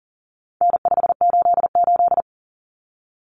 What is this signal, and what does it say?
Morse code, heard as a keyed tone, about 40 wpm, 700 Hz. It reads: N588
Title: Common Cut Numbers - 4-Digit 40wpm